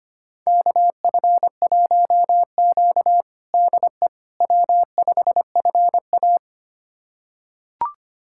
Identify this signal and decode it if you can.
Morse code, heard as a keyed tone, about 25 wpm, 700 Hz. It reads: KF1Q DE W5FA
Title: USA Callsign DE Callsign - 4 Characters - 25wpm